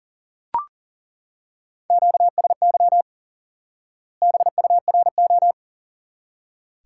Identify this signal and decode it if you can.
Morse code, heard as a keyed tone, about 40 wpm, 700 Hz. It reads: QSY BURO
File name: IQE Course 3 - Lesson 89 - YAGI - Two Elements - 40wpm